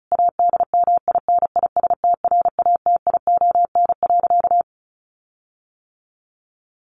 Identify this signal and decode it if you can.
Morse code, heard as a keyed tone, about 34 wpm, 700 Hz. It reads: ADMINISTRATION.
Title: Sentences from Top 1300 Words 35wpm